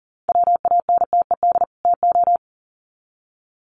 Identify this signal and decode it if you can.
Morse code, heard as a keyed tone, about 40 wpm, 700 Hz. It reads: WANTED TO
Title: Sets of 2 Words 40wpm